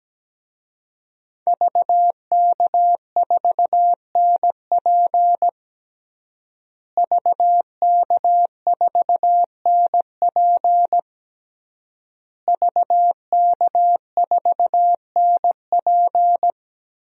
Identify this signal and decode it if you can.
Morse code, heard as a keyed tone, about 17 wpm, 700 Hz. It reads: VK4NP VK4NP VK4NP
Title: DXCC-100 Call Signs - 5 Characters - Repeated 3x - 17wpm